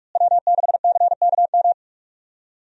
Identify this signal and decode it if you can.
Morse code, heard as a keyed tone, about 43 wpm, 700 Hz. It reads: W6CKM
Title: USA Call Signs - 5 Characters - Repeated 3x - 45wpm